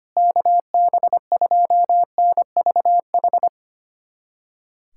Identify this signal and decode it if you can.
Morse code, heard as a keyed tone, about 25 wpm, 700 Hz. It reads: KB2NVH
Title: USA Callsign DE Callsign - 6 Characters - 25wpm